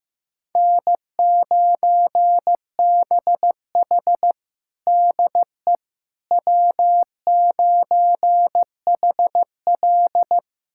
Morse code, heard as a keyed tone, about 15 wpm, 700 Hz. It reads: N9BH DE W9HL
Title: USA Callsign DE Callsign - 4 Characters - 15wpm